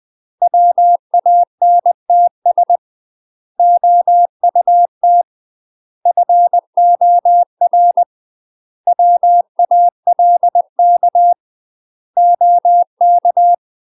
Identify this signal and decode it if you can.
Morse code, heard as a keyed tone, about 20 wpm, 700 Hz. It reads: WANTS OUT FOR WALK OK
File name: UK MASTER QSOs TRAINING / 20WPM / UK MASTER QSOs 0014